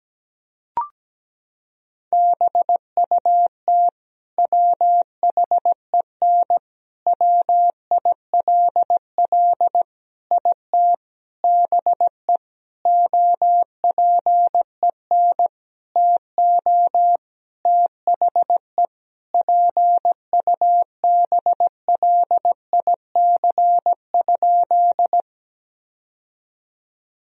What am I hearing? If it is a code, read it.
Morse code, heard as a keyed tone, about 17 wpm, 700 Hz. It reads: BUT WHEN WILL IT BE OPEN TO THE PUBLIC?
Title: Sentences from Top 1000 Words - Review Part 3 - 17wpm